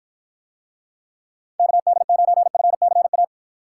Morse code, heard as a keyed tone, about 48 wpm, 700 Hz. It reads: KD9FCA